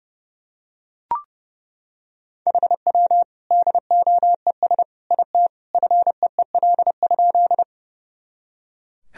Morse code, heard as a keyed tone, about 30 wpm, 700 Hz. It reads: HW DOES IT FEEL?
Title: Sentences from Top 400 Words and Abbreviations 30wpm